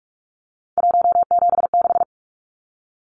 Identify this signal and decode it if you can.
Morse code, heard as a keyed tone, about 44 wpm, 700 Hz. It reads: 176
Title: Numbers - 3-Digit 45wpm